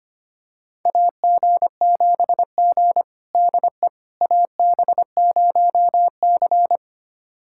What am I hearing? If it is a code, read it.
Morse code, heard as a keyed tone, about 25 wpm, 700 Hz. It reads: AG7G DE AB0C
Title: USA Callsign DE Callsign - 4 Characters - 25wpm